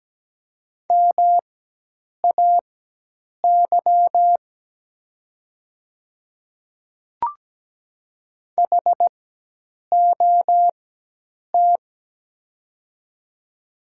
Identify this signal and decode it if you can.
Morse code, heard as a keyed tone, about 17 wpm, spaced as for 8 wpm, 700 Hz. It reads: MAY HOT
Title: 4x Character Spacing - Top 100 Words - 17wpm